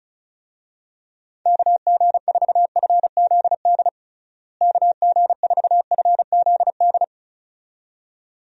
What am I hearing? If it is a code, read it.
Morse code, heard as a keyed tone, about 36 wpm, 700 Hz. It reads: KG4FZD KG4FZD
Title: USA Call Signs - 6 Characters - Repeated 2x - 35wpm